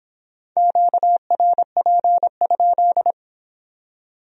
Morse code, heard as a keyed tone, about 24 wpm, 700 Hz. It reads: QRP?